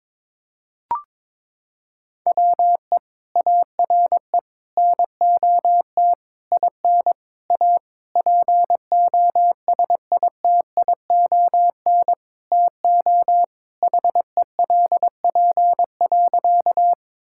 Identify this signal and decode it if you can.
Morse code, heard as a keyed tone, about 22 wpm, 700 Hz. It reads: WE ARE NOT IN A POSITION TO HELP.
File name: Sentences with 7-Word Phrases - 22wpm